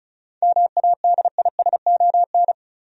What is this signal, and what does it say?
Morse code, heard as a keyed tone, about 34 wpm, 700 Hz. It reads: MADISON
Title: USA Common Names - 35wpm